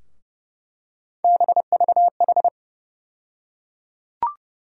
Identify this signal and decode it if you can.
Morse code, heard as a keyed tone, about 30 wpm, 700 Hz. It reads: BVH
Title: USA Call Sign Postfixes - 3 Letters 30wpm